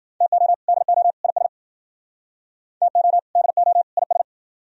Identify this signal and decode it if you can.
Morse code, heard as a keyed tone, about 51 wpm, 700 Hz. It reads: TO DO IS TO DO IS